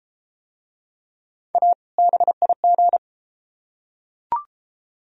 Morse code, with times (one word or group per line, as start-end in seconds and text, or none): 1.5-1.7 A
2.0-3.0 BIG